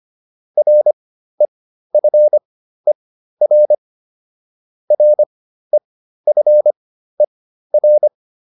Morse code, heard as a keyed tone, about 25 wpm, spaced as for 13 wpm, 600 Hz. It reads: REFER REFER